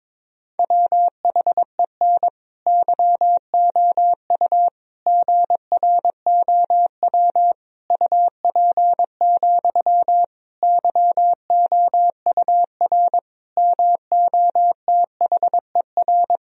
Morse code, heard as a keyed tone, about 21 wpm, 700 Hz. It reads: WHEN YOU GROW UP, YOUR MOTHER
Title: Sentences from Top 1000 Words - Review Part 1 - 22wpm